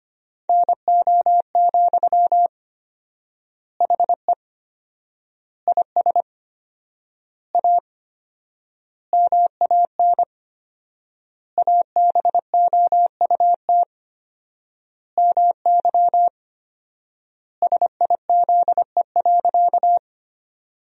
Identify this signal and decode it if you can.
Morse code, heard as a keyed tone, about 25 wpm, 700 Hz. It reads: NO, HE IS A MAN ABOUT MY SIZE.
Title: 4x Word-Spacing - Sentences from Top 400 Words - 25wpm